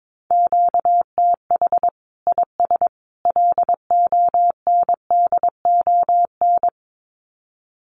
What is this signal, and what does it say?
Morse code, heard as a keyed tone, about 22 wpm, 700 Hz. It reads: QTH IS LONDON
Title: Common QSO Phrases 22wpm